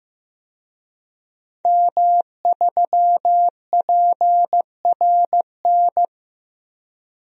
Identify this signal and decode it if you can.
Morse code, heard as a keyed tone, about 15 wpm, 700 Hz. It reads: M3PRN
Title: DXCC-100 Call Signs - 5 Characters - Repeated 2x - 15wpm